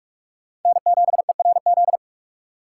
Morse code, heard as a keyed tone, about 46 wpm, 700 Hz. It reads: N7ERZ
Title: USA Call Signs - 5 Characters - 3x Character Spacing - 45wpm